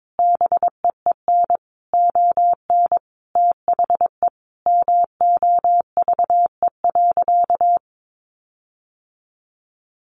Morse code, heard as a keyed tone, about 22 wpm, 700 Hz. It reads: BEEN ON THE MOVE.
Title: Sentences from Top 1400 Words 22wpm